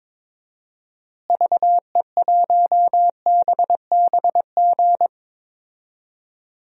Morse code, heard as a keyed tone, about 22 wpm, 700 Hz. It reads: VE1BBG
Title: DXCC-100 Call Signs - 6 Characters - Repeated 3x - 22wpm